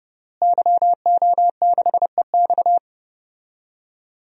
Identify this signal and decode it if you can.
Morse code, heard as a keyed tone, about 29 wpm, 700 Hz. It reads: YO6EX
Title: DXCC-100 Call Signs - 5 Characters - Repeated 3x - 30wpm